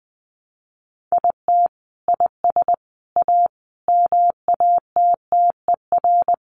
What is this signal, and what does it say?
Morse code, heard as a keyed tone, about 20 wpm, 700 Hz. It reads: IT IS A MATTER
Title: Sentences from Top 700 Words 20wpm